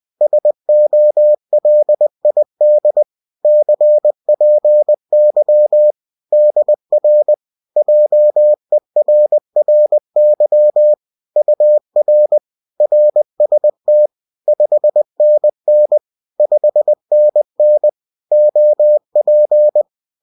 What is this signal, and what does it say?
Morse code, heard as a keyed tone, about 20 wpm, 600 Hz. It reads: SOLID CPY DR JERRY UR RST 5NN 5NN OP